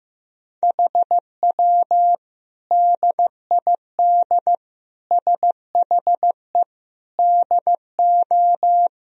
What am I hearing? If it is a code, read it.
Morse code, heard as a keyed tone, about 15 wpm, 700 Hz. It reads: HW DID SHE DO
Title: Sentences from Top 100 Words and Abbreviations 15wpm